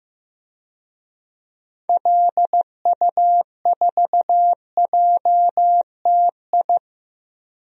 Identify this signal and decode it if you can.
Morse code, heard as a keyed tone, about 15 wpm, 700 Hz. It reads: LU4JTI
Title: DXCC-100 Call Signs - 6 Characters - 15wpm